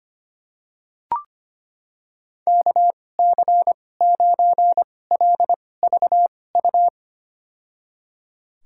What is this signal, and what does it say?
Morse code, heard as a keyed tone, about 25 wpm, spaced as for 18 wpm, 700 Hz. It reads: KC9LVU
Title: USA Call Signs - 6 Characters - 2x Character Spacing - 25wpm